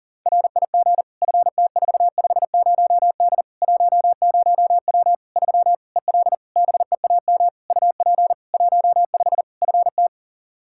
Morse code, heard as a keyed tone, about 40 wpm, 700 Hz. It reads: RIG FT450D 10W 3 EL BEAM UP 15 FT
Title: Practice QSO - Exchange 2 - Info - 40wpm